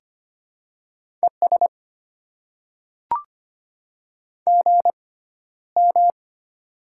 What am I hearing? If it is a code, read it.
Morse code, heard as a keyed tone, about 25 wpm, spaced as for 13 wpm, 690 Hz. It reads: ES GM
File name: Top 20 QSO Elements - 6x Character Spacing - 25wpm